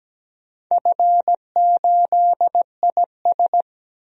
Morse code, heard as a keyed tone, about 17 wpm, 700 Hz. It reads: F8IS
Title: DXCC-100 Call Signs - 4 Characters - 17wpm